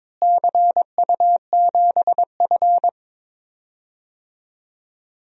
Morse code, heard as a keyed tone, about 22 wpm, 700 Hz. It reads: CU7F
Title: DXCC-100 Call Signs - 4 Characters - Repeated 2x - 22wpm